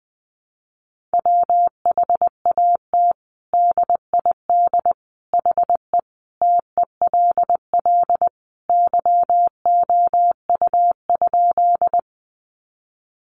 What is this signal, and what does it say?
Morse code, heard as a keyed tone, about 20 wpm, 700 Hz. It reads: WHAT DID HE TELL YOU?
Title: Sentences from Top 200 Words 20wpm